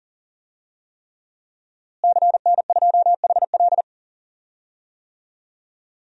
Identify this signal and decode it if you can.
Morse code, heard as a keyed tone, about 37 wpm, 700 Hz. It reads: CN2HL
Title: DXCC-100 Call Signs - 5 Characters - Repeated 3x - 40wpm